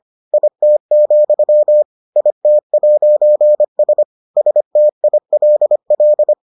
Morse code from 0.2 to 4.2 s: IT, IT'S

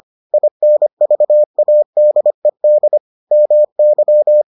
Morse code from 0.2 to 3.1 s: INVADED

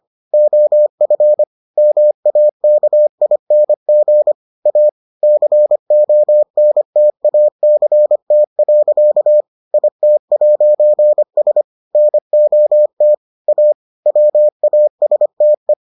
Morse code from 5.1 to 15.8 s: CONTACT. IT'S NOT A WASTE

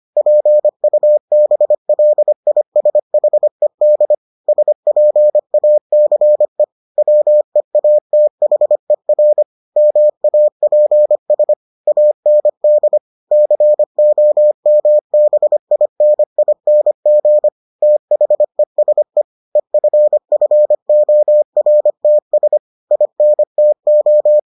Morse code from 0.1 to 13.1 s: PUBLISHED SPACE WEATHER MAPS AND

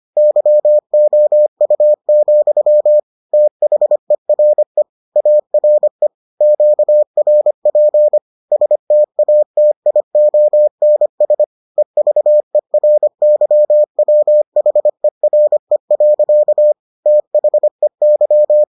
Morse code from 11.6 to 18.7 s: EVERYWHERE. THEY

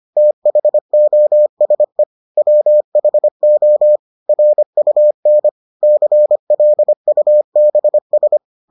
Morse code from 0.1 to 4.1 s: THOSE WHO

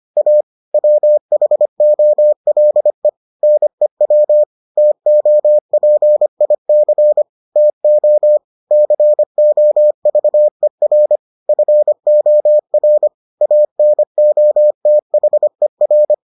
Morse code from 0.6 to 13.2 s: WHOLE NEW TOPIC TO COVER FOR

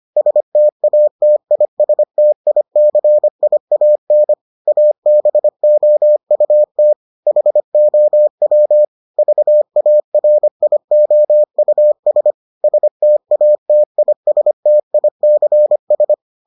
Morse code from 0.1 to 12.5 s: STATISTICIAN ABOUT HOW VARIOUS

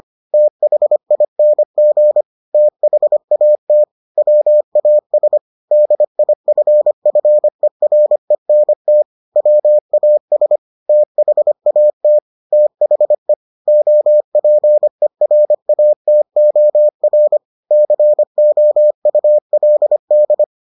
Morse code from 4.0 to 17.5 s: WAS DIFFERENT WAS THAT THE OPERATOR

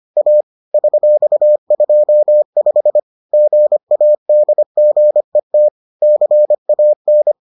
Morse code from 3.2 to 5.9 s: GADGET

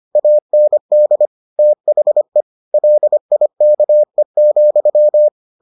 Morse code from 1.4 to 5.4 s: THE LIKE,

